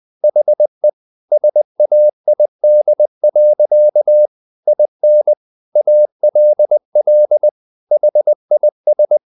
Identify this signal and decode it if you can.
Morse code, heard as a keyed tone, about 20 wpm, 600 Hz. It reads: HE SAID. IN ALL HIS
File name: the old man and the sea / the old man and the sea part 1 20wpm